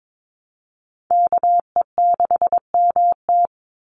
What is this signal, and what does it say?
Morse code, heard as a keyed tone, about 22 wpm, 700 Hz. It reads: KE6MT
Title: NA SOTA Chaser Callsigns 22wpm